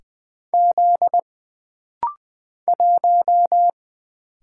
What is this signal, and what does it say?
Morse code, heard as a keyed tone, about 20 wpm, 700 Hz. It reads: Z 1
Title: Single Letter-Number Alt - Rapid-Fire 20wpm